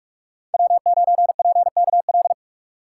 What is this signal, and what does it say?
Morse code, heard as a keyed tone, about 41 wpm, 700 Hz. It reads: W9PKL
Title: USA Call Signs - 5 Characters - 4x Character Spacing - 45wpm